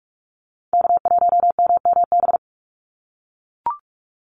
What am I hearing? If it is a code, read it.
Morse code, heard as a keyed tone, about 44 wpm, 700 Hz. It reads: K1MMB
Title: Call Signs - 45wpm